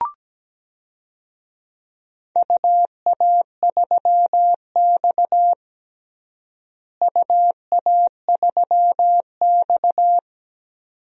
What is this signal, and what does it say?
Morse code, heard as a keyed tone, about 17 wpm, 700 Hz. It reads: UA3X UA3X
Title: DXCC-100 Call Signs - 4 Characters - Repeated 2x - 17wpm